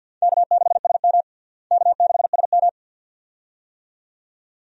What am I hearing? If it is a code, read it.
Morse code, heard as a keyed tone, about 48 wpm, 700 Hz. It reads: K6SM K6SM
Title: USA Call Signs - 4 Characters - Repeated 2x - 50wpm